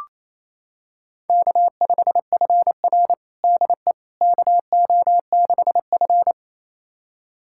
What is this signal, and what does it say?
Morse code, heard as a keyed tone, about 27 wpm, 700 Hz. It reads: K5FR DE KO6F
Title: USA Callsign DE Callsign - 4 Characters - 28wpm